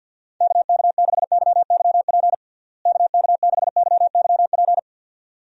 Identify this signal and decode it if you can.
Morse code, heard as a keyed tone, about 51 wpm, 700 Hz. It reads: KK6YYP KK6YYP
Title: USA Call Signs - 6 Characters - Repeated 2x - 50wpm